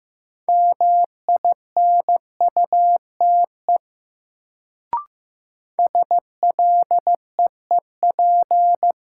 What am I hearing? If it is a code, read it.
Morse code, heard as a keyed tone, about 15 wpm, 700 Hz. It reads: MINUTE SLEEP
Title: Sentences from Top 500 Words - New Word Drill - 15wpm